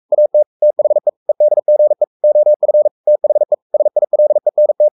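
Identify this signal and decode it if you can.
Morse code, heard as a keyed tone, about 44 wpm, 600 Hz. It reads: AT THE EDGE OF THE SILENT
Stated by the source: Morse code at 43 wpm sending Ken Follett's novel Never in English, from CW med Gnister.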